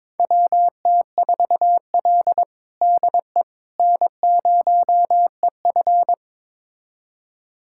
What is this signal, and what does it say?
Morse code, heard as a keyed tone, about 22 wpm, 700 Hz. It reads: WT4L DE N0EF
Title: USA Callsign DE Callsign - 4 Characters - 22wpm